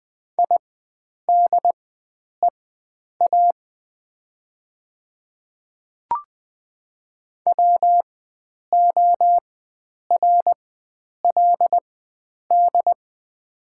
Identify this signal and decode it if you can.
Morse code, heard as a keyed tone, about 20 wpm, spaced as for 9 wpm, 700 Hz. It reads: IDEA WORLD
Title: 4x Character Spacing - Top 500 Words - 20wpm